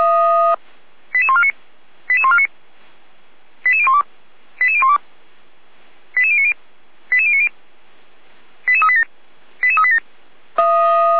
Five-tone selective call (ZVEI), heard as quick runs of five tones, tones of about 70 ms. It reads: 88149 88149 88012 88012 88090 88090 88387 88387
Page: ZVEI